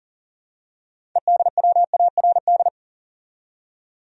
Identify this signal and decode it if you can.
Morse code, heard as a keyed tone, about 40 wpm, 700 Hz. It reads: EDWARD
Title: USA Common Names - 40wpm